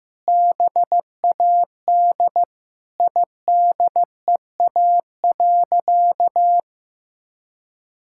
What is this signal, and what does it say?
Morse code, heard as a keyed tone, about 15 wpm, 700 Hz. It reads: BAD IDEA.
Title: Sentences from Top 1000 Words - Review Part 5 - 15wpm